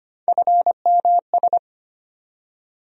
Morse code, heard as a keyed tone, about 25 wpm, 700 Hz. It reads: FMS
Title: USA Call Sign Postfixes - 3 Letters 25wpm